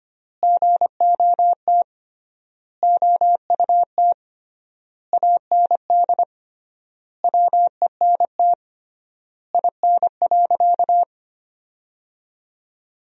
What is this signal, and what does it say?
Morse code, heard as a keyed tone, about 25 wpm, 700 Hz. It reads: GOT OUT AND WENT IN.
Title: 3x Word-Spacing - Sentences from Top 300 Words - 25wpm